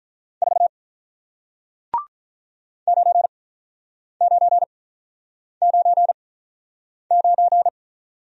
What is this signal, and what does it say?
Morse code, heard as a keyed tone, about 45 wpm, spaced as for 9 wpm, 700 Hz. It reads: 4 9999